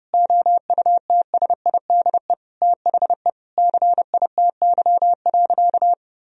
Morse code, heard as a keyed tone, about 32 wpm, 700 Hz. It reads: OUTSIDE THE CITY.
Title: Sentences from Top 1000 Words - Review Part 4 - 30wpm